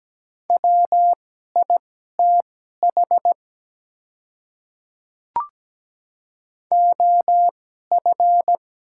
Morse code, heard as a keyed tone, about 17 wpm, spaced as for 12 wpm, 700 Hz. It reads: WITH OF
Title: 2x Character Spacing - Top 100 Words - 17wpm